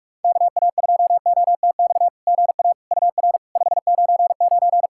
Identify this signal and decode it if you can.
Morse code, heard as a keyed tone, about 46 wpm, 700 Hz. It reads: KA2OTX GA UR 599
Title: SOTA QSO Practice 45wpm